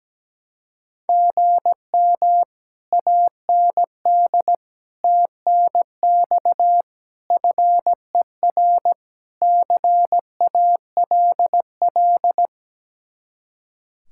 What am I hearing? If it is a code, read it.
Morse code, heard as a keyed tone, about 18 wpm, 700 Hz. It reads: GM AND TNX FER CALL
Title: Speed-Racing Common QSO Phrases - 17wpm